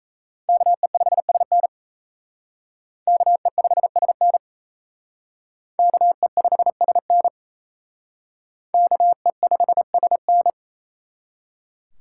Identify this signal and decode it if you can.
Morse code, heard as a keyed tone, about 34 wpm, 700 Hz. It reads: KE5SN KE5SN KE5SN KE5SN
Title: Speed Racing - USA Call Signs 28wpm